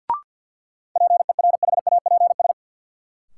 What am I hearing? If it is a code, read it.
Morse code, heard as a keyed tone, about 51 wpm, 700 Hz. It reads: PERHAPS
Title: Sentences from Top 500 Words - New Word Drill - 50wpm